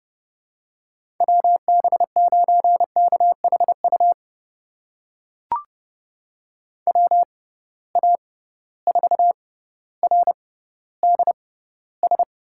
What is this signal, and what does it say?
Morse code, heard as a keyed tone, about 30 wpm, spaced as for 15 wpm, 700 Hz. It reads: WB9KHU WA4RDS
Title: USA Call Signs - 6 Characters - 6x Character Spacing - 30wpm